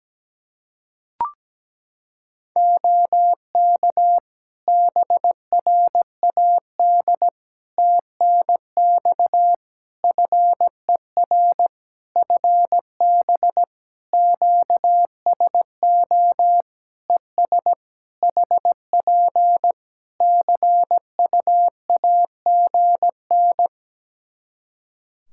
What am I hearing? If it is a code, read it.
Morse code, heard as a keyed tone, about 17 wpm, 700 Hz. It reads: OK BRAD TNX FER FB QSO ES HP CUAGN
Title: K1USN SST QSO Practice 17wpm